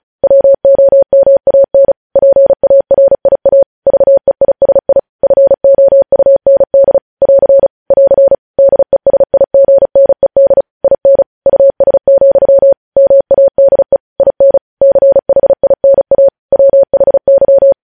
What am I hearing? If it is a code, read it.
Morse code, heard as a keyed tone, about 36 wpm, 550 Hz. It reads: WOMAN PARIA VEISI FOUND <AR> <AR> DESIGNED IN US, MADE IN CHINA WHY
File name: podcast / News headlines-1x-35x35